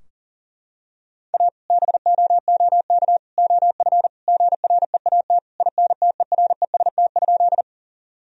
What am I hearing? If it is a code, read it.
Morse code, heard as a keyed tone, about 37 wpm, 700 Hz. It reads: A BOOK OF GREAT INTEREST?
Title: Sentences from Top 1000 Words - Review Part 2 - 40wpm